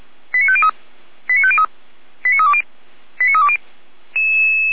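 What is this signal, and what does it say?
Five-tone selective call (ZVEI), heard as quick runs of five tones, tones of about 70 ms. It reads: 89503 89503 89320 89320
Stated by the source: ZVEI five-tone selective call 89315, received on SDR; audio hosted by feuerwehr-orlamuende (fire brigade Orlamünde).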